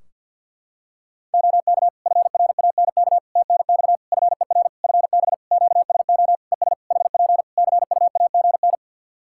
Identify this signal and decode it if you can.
Morse code, heard as a keyed tone, about 48 wpm, 700 Hz. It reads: OK FRANK TNX FER FB QSO ES HP CUAGN
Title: Practice QSO - Exchange 4 - Ending - 50wpm